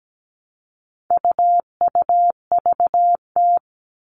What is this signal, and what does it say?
Morse code, heard as a keyed tone, about 17 wpm, 700 Hz. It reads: UUVT
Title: All Cut Numbers - 4-Digit 17wpm